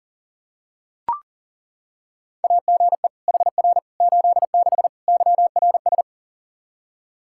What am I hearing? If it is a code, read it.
Morse code, heard as a keyed tone, about 40 wpm, 700 Hz. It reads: AGE HR 86 YRS
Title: Practice QSO - Exchange 3 - More Info - 40wpm